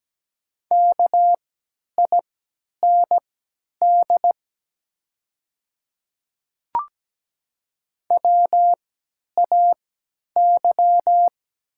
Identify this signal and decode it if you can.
Morse code, heard as a keyed tone, about 17 wpm, spaced as for 10 wpm, 700 Hz. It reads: KIND WAY
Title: 3x Character Spacing - Top 200 Words - 17wpm